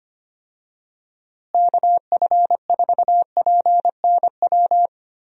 Morse code, heard as a keyed tone, about 25 wpm, 700 Hz. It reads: KF4PNW
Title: USA Call Signs - 6 Characters - Repeated 2x - 25wpm